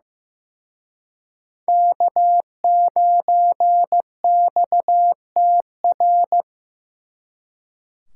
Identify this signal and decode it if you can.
Morse code, heard as a keyed tone, about 18 wpm, 700 Hz. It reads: K9XTR
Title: Speed Racing - USA 1x3 Call Signs 15wpm